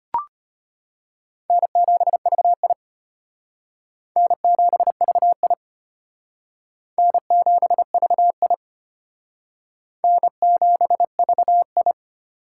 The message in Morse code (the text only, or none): N7VI N7VI N7VI N7VI